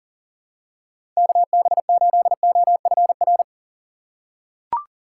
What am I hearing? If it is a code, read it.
Morse code, heard as a keyed tone, about 40 wpm, 700 Hz. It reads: KB8OFR
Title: USA Call Signs - 6 Characters - 2x Character Spacing - 40wpm